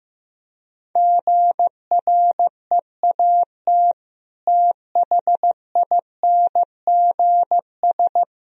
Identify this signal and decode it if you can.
Morse code, heard as a keyed tone, about 15 wpm, 700 Hz. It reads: GREAT THINGS
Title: Sentences from Top 1000 Words - Review Part 4 - 15wpm